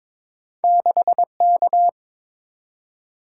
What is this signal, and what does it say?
Morse code, heard as a keyed tone, about 22 wpm, 700 Hz. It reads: 6K
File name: practice / 22wpm / Two Letter-Number Combinations - 22wpm